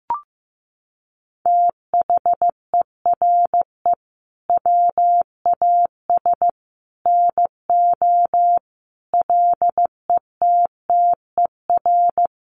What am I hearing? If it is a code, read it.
Morse code, heard as a keyed tone, about 15 wpm, 700 Hz. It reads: THERE WAS NO LETTER
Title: Sentences from Top 300 Words 15wpm